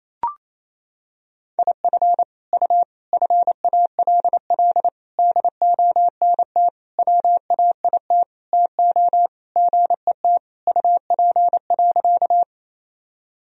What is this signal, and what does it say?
Morse code, heard as a keyed tone, about 28 wpm, 700 Hz. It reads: IF U FALL DONT WAIT TO GET UP.
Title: Sentences from Top 500 Words and Abbreviations 28wpm